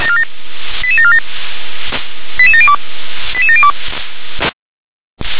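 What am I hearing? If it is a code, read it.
Five-tone selective call (ZVEI), heard as quick runs of five tones, tones of about 70 ms. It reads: 88747 88702 88702
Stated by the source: ZVEI five-tone selective call received on an SDR.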